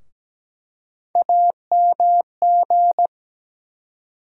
Morse code, heard as a keyed tone, about 17 wpm, 700 Hz. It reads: AMG
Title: USA Call Sign Postfixes - 3 Letters 17wpm